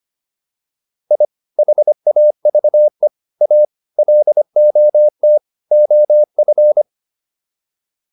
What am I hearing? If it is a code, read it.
Morse code, heard as a keyed tone, about 25 wpm, 600 Hz. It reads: I HAVE A LOT OF